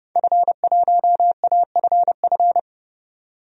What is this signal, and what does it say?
Morse code, heard as a keyed tone, about 30 wpm, 700 Hz. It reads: F1AFF
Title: DXCC-100 Call Signs - 5 Characters - 30wpm